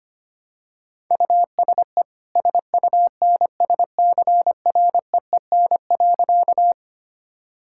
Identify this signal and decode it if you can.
Morse code, heard as a keyed tone, about 25 wpm, 700 Hz. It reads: USE SUNSCREEN.
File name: Easy Sentences with Hints - 25wpm